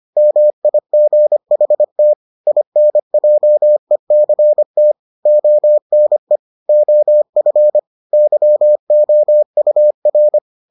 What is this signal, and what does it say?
Morse code, heard as a keyed tone, about 25 wpm, 600 Hz. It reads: MIGHT INJECT ONE OF YOUR